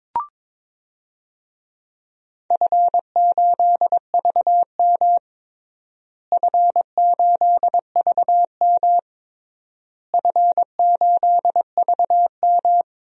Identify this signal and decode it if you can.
Morse code, heard as a keyed tone, about 22 wpm, 700 Hz. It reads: F8VM F8VM F8VM
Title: DXCC-100 Call Signs - 4 Characters - Repeated 3x - 22wpm